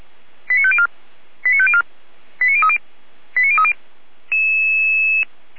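Five-tone selective call (ZVEI), heard as quick runs of five tones, tones of about 70 ms. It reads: 89504 89504 89030 89030
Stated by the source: Received on an SDR.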